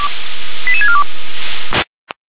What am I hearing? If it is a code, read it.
Five-tone selective call (ZVEI), heard as quick runs of five tones, tones of about 70 ms. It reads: 88642
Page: ZVEI